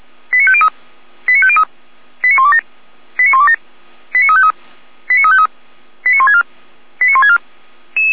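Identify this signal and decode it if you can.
Five-tone selective call (ZVEI), heard as quick runs of five tones, tones of about 70 ms. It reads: 89503 89503 89127 89127 89353 89353 89164 89164
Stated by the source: ZVEI five-tone selective call received by SDR.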